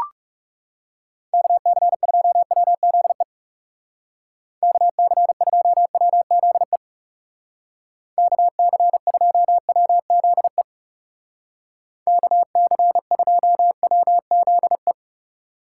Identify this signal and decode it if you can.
Morse code, heard as a keyed tone, about 38 wpm, 700 Hz. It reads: KC2WZE KC2WZE KC2WZE KC2WZE